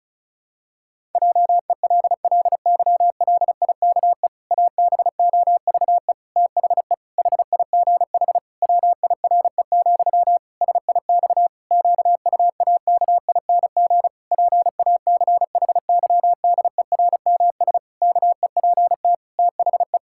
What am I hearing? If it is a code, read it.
Morse code, heard as a keyed tone, about 34 wpm, 700 Hz. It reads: JELLYLIKE ABOVE THE HIGH WIRE, SIX QUAKING PACHYDERMS KEPT THE
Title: Pangram Challenge v2 - 35wpm